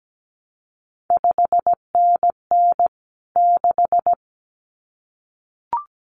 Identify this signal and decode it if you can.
Morse code, heard as a keyed tone, about 17 wpm, 700 Hz. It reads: 5NN 6
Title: CQ WW DX Contest Exchange Encore - 17wpm